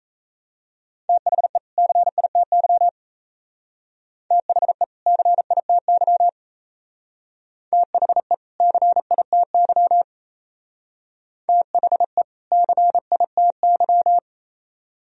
Morse code, heard as a keyed tone, about 33 wpm, 700 Hz. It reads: THE CITY THE CITY THE CITY THE CITY
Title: Speed-Racing - Sets of 2 Words 28wpm